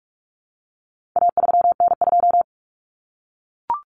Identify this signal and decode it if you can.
Morse code, heard as a keyed tone, about 44 wpm, 700 Hz. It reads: A3N2
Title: Common Cut Numbers - 4-Digit 45wpm